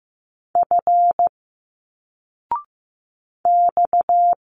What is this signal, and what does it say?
Morse code, heard as a keyed tone, about 15 wpm, 700 Hz. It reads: F X
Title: Single Letters 15wpm